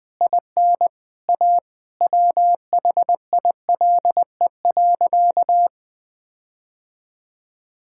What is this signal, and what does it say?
Morse code, heard as a keyed tone, about 19 wpm, 700 Hz. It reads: IN A WHILE.
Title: Sentences with 4-Word Phrases - 20wpm